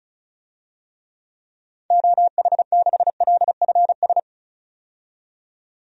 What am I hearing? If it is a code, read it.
Morse code, heard as a keyed tone, about 34 wpm, 700 Hz. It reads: OH6LFS
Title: DXCC-100 Call Signs - 6 Characters - 35wpm